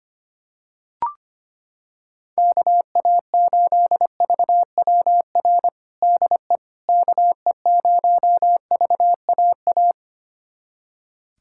Morse code, heard as a keyed tone, about 25 wpm, 700 Hz. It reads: KA8VWR DE KE0VAA
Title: USA Callsign DE Callsign - 6 Characters - 25wpm